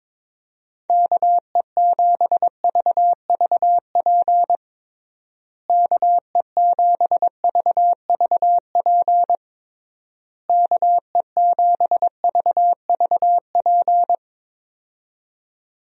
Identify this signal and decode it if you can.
Morse code, heard as a keyed tone, about 22 wpm, 700 Hz. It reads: KE7VVP KE7VVP KE7VVP
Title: USA Call Signs - 6 Characters - Repeated 3x - 22wpm